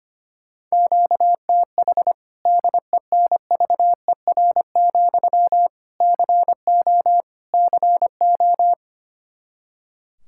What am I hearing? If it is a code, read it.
Morse code, heard as a keyed tone, about 25 wpm, 700 Hz. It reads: QTH DENVER, CO CO
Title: Practice QSO - Exchange 1 - Beginning - 25wpm